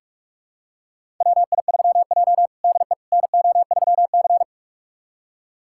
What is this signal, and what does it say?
Morse code, heard as a keyed tone, about 46 wpm, 700 Hz. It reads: WI3J DE NO3C
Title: USA Callsign DE Callsign - 4 Characters - 45wpm